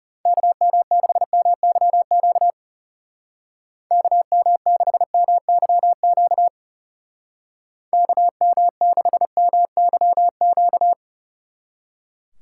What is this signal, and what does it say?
Morse code, heard as a keyed tone, about 37 wpm, 700 Hz. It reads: KM6MYQ KM6MYQ KM6MYQ
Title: Speed Racing - USA 2x3 Call Signs 30wpm